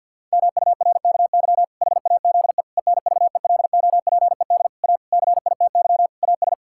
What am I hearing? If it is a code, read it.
Morse code, heard as a keyed tone, about 52 wpm, 700 Hz. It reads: MURKY HAZE ENVELOPED A CITY AS
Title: Pangram Challenge v2 - 50wpm